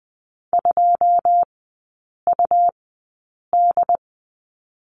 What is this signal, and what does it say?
Morse code, heard as a keyed tone, about 20 wpm, spaced as for 8 wpm, 700 Hz. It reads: 2UD
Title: 152 Lesson 25e - Letter p - Three Characters 20wpm